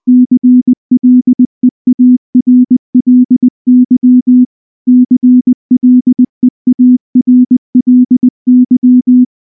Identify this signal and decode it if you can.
Morse code, heard as a keyed tone, about 19 wpm, 250 Hz. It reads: CLEARLY CLEARLY
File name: morse / morse-training-20WPM-5